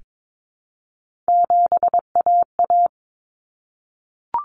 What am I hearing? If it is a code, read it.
Morse code, heard as a keyed tone, about 22 wpm, 700 Hz. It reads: 7AA
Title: Common Cut Numbers - 3-Digit 22wpm